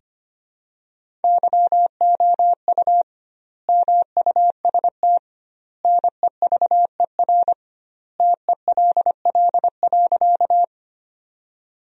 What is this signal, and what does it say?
Morse code, heard as a keyed tone, about 25 wpm, 700 Hz. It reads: YOU MUST NEVER TELL.
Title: 2x Word-Spacing - Sentences from Top 300 Words - 25wpm